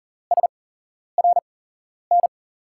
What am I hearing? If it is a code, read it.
Morse code, heard as a keyed tone, about 39 wpm, spaced as for 11 wpm, 700 Hz. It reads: SRN
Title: USA Call Signs - 6 Characters - 8x Character Spacing - 40wpm